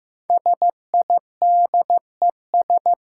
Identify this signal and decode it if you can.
Morse code, heard as a keyed tone, about 15 wpm, 700 Hz. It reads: SIDES
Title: Sets of 4 Words - New Word Drill - 15wpm